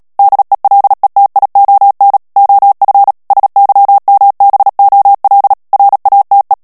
Morse code, heard as a keyed tone, about 37 wpm, 800 Hz. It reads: DELETION OF SYMBOL RATE
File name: morse mp3 / ANGLAIS texte clair 3 vitesse 35 mots minute